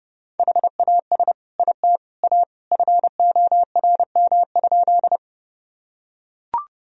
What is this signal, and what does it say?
Morse code, heard as a keyed tone, about 30 wpm, 700 Hz. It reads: HAS IT A FORM?
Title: Sentences from Top 1000 Words - Review Part 1 - 30wpm